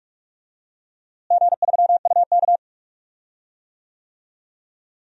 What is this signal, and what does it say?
Morse code, heard as a keyed tone, about 42 wpm, 700 Hz. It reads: G3UK
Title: DXCC-100 Call Signs - 4 Characters - Repeated 3x - 45wpm